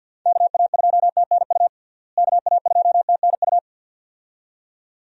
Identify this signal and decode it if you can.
Morse code, heard as a keyed tone, about 52 wpm, 700 Hz. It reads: KA2TNU KA2TNU
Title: USA Call Signs - 6 Characters - Repeated 2x - 50wpm